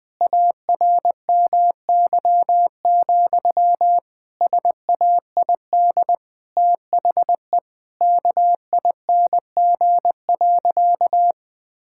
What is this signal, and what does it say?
Morse code, heard as a keyed tone, about 20 wpm, 700 Hz. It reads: ARMY, SAID THE KING.
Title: Sentences from Top 1000 Words - Review Part 5 - 20wpm